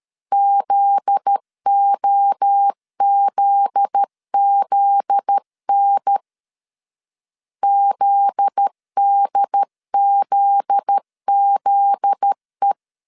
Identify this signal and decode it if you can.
Morse code, heard as a keyed tone, about 13 wpm, 800 Hz. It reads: ZOZZN ZDZZE
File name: LEZIONI / 7 GF9Z / 7 GF9Z 12wpm 2147